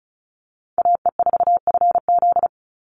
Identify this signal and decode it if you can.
Morse code, heard as a keyed tone, about 34 wpm, 700 Hz. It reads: AE4FZ
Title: NA SOTA Chaser Callsigns 35wpm